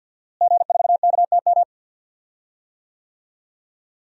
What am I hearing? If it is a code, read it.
Morse code, heard as a keyed tone, about 46 wpm, 700 Hz. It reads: G4KTM
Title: DXCC-100 Call Signs - 5 Characters - Repeated 3x - 50wpm